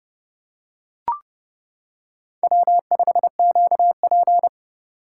Morse code, heard as a keyed tone, about 32 wpm, 700 Hz. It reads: W5QP
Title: K1USN SST QSO Practice - 30wpm